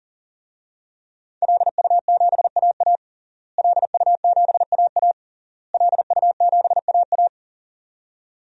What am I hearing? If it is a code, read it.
Morse code, heard as a keyed tone, about 40 wpm, 700 Hz. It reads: LU7AA LU7AA LU7AA
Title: DXCC-100 Call Signs - 5 Characters - Repeated 3x - 40wpm